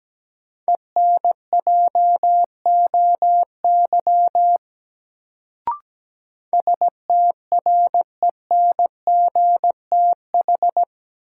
Morse code, heard as a keyed tone, about 17 wpm, 700 Hz. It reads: ENJOY STRENGTH
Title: Sentences from Top 1300 Words - New Word Drill Alt - 17wpm